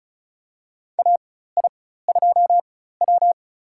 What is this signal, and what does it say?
Morse code, heard as a keyed tone, about 36 wpm, spaced as for 16 wpm, 700 Hz. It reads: AI2W
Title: USA Call Signs - 4 Characters - 4x Character Spacing - 35wpm